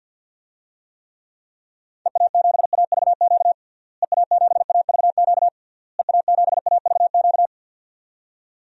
Morse code, heard as a keyed tone, about 48 wpm, 700 Hz. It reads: EA7AVQ EA7AVQ EA7AVQ
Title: DXCC-100 Call Signs - 6 Characters - Repeated 3x - 50wpm